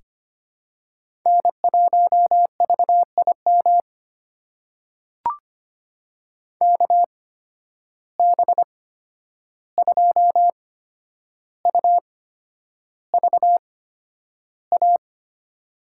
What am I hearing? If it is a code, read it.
Morse code, heard as a keyed tone, about 25 wpm, spaced as for 10 wpm, 700 Hz. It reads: N1VIM KB2UVA
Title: USA Call Signs - 8x Character Spacing - 25wpm